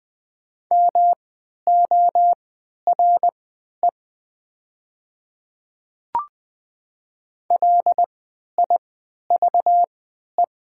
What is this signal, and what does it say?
Morse code, heard as a keyed tone, about 20 wpm, spaced as for 11 wpm, 690 Hz. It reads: MORE LIVE
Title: 3x Character Spacing - Top 200 Words - 20wpm